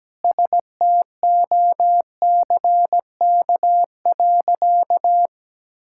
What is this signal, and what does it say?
Morse code, heard as a keyed tone, about 17 wpm, 700 Hz. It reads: STOCK.
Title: Sentences from Top 1000 Words - Review Part 5 - 17wpm